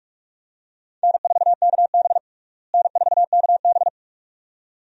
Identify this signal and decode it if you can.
Morse code, heard as a keyed tone, about 46 wpm, 700 Hz. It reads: N4KB N4KB
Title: USA Call Signs - 4 Characters - Repeated 2x - 45wpm